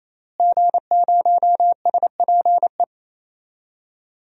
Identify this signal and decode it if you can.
Morse code, heard as a keyed tone, about 28 wpm, 700 Hz. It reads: G0SPE
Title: DXCC-100 Call Signs - 5 Characters - Repeated 3x - 28wpm